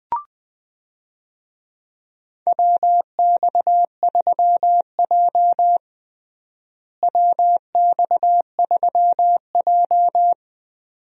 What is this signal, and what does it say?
Morse code, heard as a keyed tone, about 20 wpm, 700 Hz. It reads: WX3J WX3J
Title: USA Call Signs - 4 Characters - Repeated 2x - 20wpm